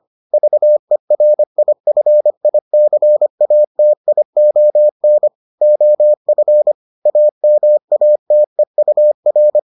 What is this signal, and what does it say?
Morse code, heard as a keyed tone, about 25 wpm, 600 Hz. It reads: VERIFICATION OF AMATEUR